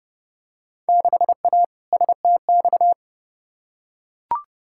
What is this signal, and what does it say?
Morse code, heard as a keyed tone, about 30 wpm, 700 Hz. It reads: 6A STX
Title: Field Day Exchanges - 30wpm